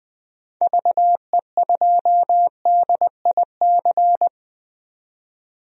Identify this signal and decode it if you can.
Morse code, heard as a keyed tone, about 20 wpm, 700 Hz. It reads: VE2DIC
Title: DXCC-100 Call Signs - 6 Characters - 20wpm